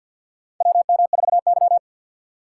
Morse code, heard as a keyed tone, about 52 wpm, 700 Hz. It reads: WM4Y
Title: USA Call Signs - 4 Characters - 2x Character Spacing - 50wpm